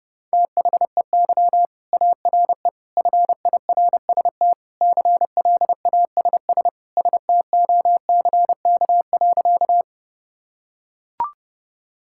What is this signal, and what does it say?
Morse code, heard as a keyed tone, about 30 wpm, 700 Hz. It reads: THEY ARE FIRST CLASS STOCK.
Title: Sentences from Top 1000 Words - Review Part 5 - 30wpm